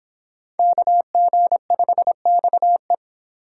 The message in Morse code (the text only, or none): KG5XE